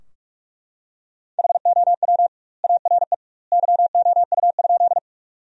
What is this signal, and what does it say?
Morse code, heard as a keyed tone, about 43 wpm, 700 Hz. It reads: HOW ARE YOU?
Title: 2x Word-Spacing - Sentences from Top 100 Words - 45wpm